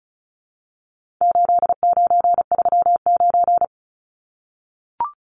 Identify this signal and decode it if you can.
Morse code, heard as a keyed tone, about 34 wpm, 700 Hz. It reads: 8939
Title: Numbers - 4-Digit 35wpm